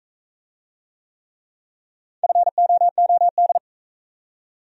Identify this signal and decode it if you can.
Morse code, heard as a keyed tone, about 45 wpm, 700 Hz. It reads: FOOD